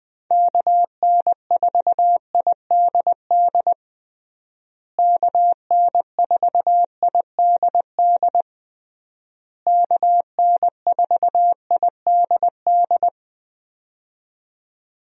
Morse code, heard as a keyed tone, about 20 wpm, 700 Hz. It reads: KN4IDD KN4IDD KN4IDD
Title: USA Call Signs - 6 Characters - Repeated 3x - 20wpm